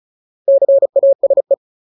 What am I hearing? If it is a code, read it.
Morse code, heard as a keyed tone, about 35 wpm, 550 Hz. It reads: CASE
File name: Ultimate Word List / 35WPM Ultimate